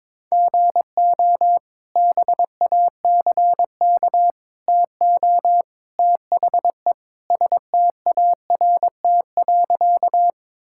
Morse code, heard as a keyed tone, about 22 wpm, 700 Hz. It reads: GO BACK TO THE START.
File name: practice / 22wpm / Sentences with 3-Word Phrases - 22wpm